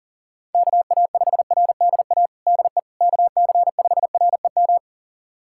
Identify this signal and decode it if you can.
Morse code, heard as a keyed tone, about 40 wpm, 700 Hz. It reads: KA5RDA DE KC5REM
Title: USA Callsign DE Callsign 40wpm